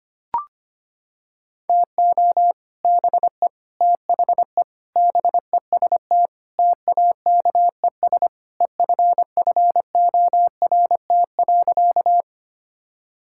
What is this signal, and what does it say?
Morse code, heard as a keyed tone, about 25 wpm, 700 Hz. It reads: TO BE THE BEST TAKES EFFORT.
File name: practice / 25wpm / Sentences with 3-Word Phrases - 25wpm